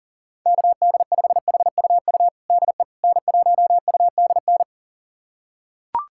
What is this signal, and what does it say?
Morse code, heard as a keyed tone, about 40 wpm, 700 Hz. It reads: KD5HUU DE N1UDN
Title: USA Callsign DE Callsign 40wpm